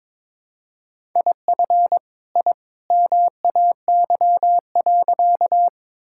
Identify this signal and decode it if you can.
Morse code, heard as a keyed tone, about 22 wpm, 700 Hz. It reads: IF I MAY.
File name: practice / 22wpm / Sentences with 2-Word Phrases - 22wpm